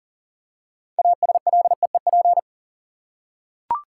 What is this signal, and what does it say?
Morse code, heard as a keyed tone, about 40 wpm, 700 Hz. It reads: ASLEEP
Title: Six Letter Words - 40wpm